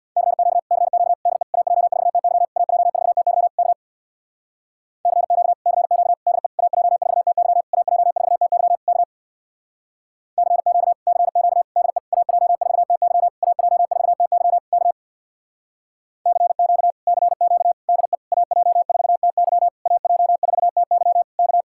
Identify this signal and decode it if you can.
Morse code, heard as a keyed tone, about 62 wpm, 700 Hz. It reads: CQ CQ DE AJ4TY AJ4TY K CQ CQ DE AJ4TY AJ4TY K CQ CQ DE AJ4TY AJ4TY K CQ CQ DE AJ4TY AJ4TY K
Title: Speed-Racing Common QSO Phrases - 50wpm